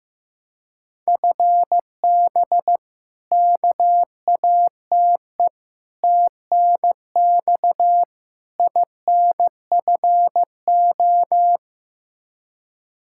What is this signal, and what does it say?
Morse code, heard as a keyed tone, about 15 wpm, 700 Hz. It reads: FB KATE TNX INFO
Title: Practice QSO - Exchange 3 - More Info - 15wpm